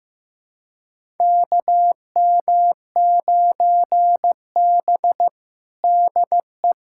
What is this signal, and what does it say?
Morse code, heard as a keyed tone, about 15 wpm, 700 Hz. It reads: KM9B DE
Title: USA Callsign DE Callsign - 4 Characters - 15wpm